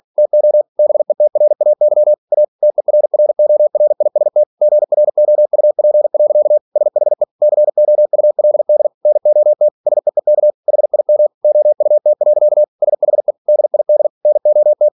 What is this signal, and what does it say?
Morse code, heard as a keyed tone, about 46 wpm, 600 Hz. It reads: TO BETRAY A TERRORIST GROUP. SHE COULD NOT SEEK HIM OUT. SHE DID NOT
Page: Never 47wpm